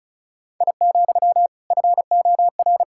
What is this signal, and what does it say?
Morse code, heard as a keyed tone, about 36 wpm, 700 Hz. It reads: I, FOR
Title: Sentences from Top 1000 Words - Review Part 5 - 35wpm